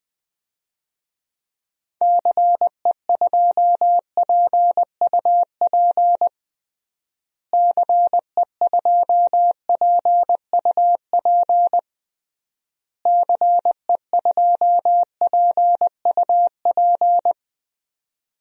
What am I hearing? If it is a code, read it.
Morse code, heard as a keyed tone, about 20 wpm, 700 Hz. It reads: CE2PUP CE2PUP CE2PUP
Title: DXCC-100 Call Signs - 6 Characters - Repeated 3x - 20wpm